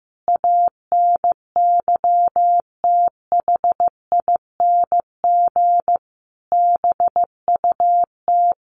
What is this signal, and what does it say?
Morse code, heard as a keyed tone, about 15 wpm, 700 Hz. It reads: ANYTHING BUT
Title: Sentences from Top 1000 Words 15wpm